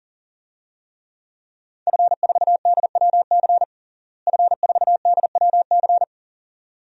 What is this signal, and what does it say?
Morse code, heard as a keyed tone, about 40 wpm, 700 Hz. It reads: F4DWC F4DWC
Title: DXCC-100 Call Signs - 5 Characters - Repeated 2x - 40wpm